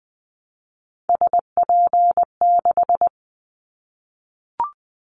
Morse code, heard as a keyed tone, about 20 wpm, 700 Hz. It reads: SP6
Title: Callsign Prefixes for Top 25 DXCC - 20wpm